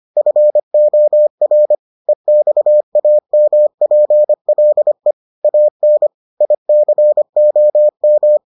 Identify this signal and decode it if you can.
Morse code, heard as a keyed tone, about 25 wpm, 600 Hz. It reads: FOR EXAMPLE AN ICOM